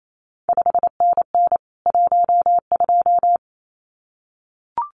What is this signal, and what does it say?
Morse code, heard as a keyed tone, about 28 wpm, 700 Hz. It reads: 5NN 12
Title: CQ WW DX Contest Exchange Encore - 28wpm